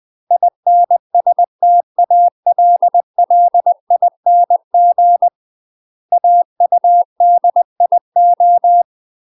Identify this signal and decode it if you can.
Morse code, heard as a keyed tone, about 20 wpm, 700 Hz. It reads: INSTALLING AUDIO
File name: UK MASTER QSOs TRAINING/20WPM